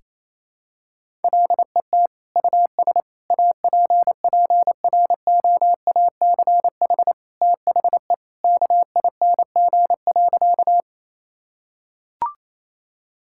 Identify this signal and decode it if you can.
Morse code, heard as a keyed tone, about 27 wpm, 700 Hz. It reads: LET US APPROACH THE KING.
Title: Sentences from Top 1000 Words - Review Part 6 - 28wpm